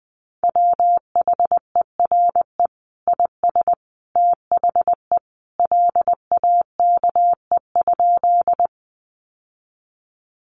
Morse code, heard as a keyed tone, about 20 wpm, 700 Hz. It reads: WHERE IS THE LAKE?
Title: Sentences from Top 1100 Words 20wpm